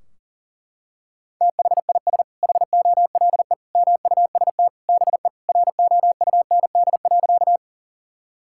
Morse code, heard as a keyed tone, about 37 wpm, 700 Hz. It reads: THIS HOLE MUST BE ROUND.
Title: Sentences from Top 1600 Words 40wpm